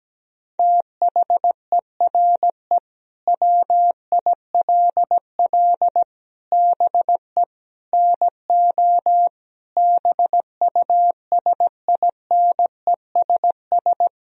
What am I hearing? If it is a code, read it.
Morse code, heard as a keyed tone, about 17 wpm, 700 Hz. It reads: THERE WILL BE NO BUSINESS
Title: Sentences from Top 1000 Words - Review Part 4 - 17wpm